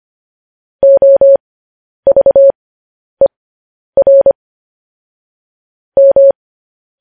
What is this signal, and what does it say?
Morse code, heard as a keyed tone, about 25 wpm, spaced as for 10 wpm, 550 Hz. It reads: OVER M
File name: podcast / News headlines-1x-25x10